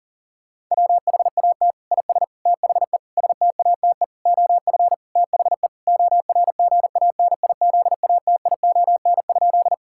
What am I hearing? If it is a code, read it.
Morse code, heard as a keyed tone, about 40 wpm, 700 Hz. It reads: WHAT IS THE STATE OF THE ORGANIZATION?
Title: Sentences from Top 1000 Words - Review Part 5 - 40wpm